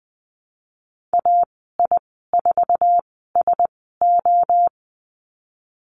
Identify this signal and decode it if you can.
Morse code, heard as a keyed tone, about 20 wpm, spaced as for 15 wpm, 700 Hz. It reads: AI4SO
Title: 055 Lesson 10g - Letter r - Call Signs 20wpm